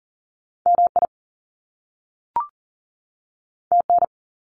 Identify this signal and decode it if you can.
Morse code, heard as a keyed tone, about 41 wpm, 700 Hz. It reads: MI TN
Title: US States 40wpm